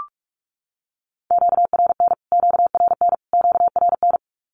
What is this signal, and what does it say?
Morse code, heard as a keyed tone, about 44 wpm, 700 Hz. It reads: QRN QRN QRN
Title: Q-Codes Repeated 3x 45wpm